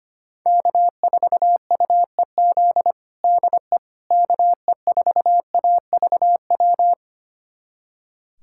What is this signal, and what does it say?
Morse code, heard as a keyed tone, about 25 wpm, 700 Hz. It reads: K4UEZ DE KE4AVW
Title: USA Callsign DE Callsign 25wpm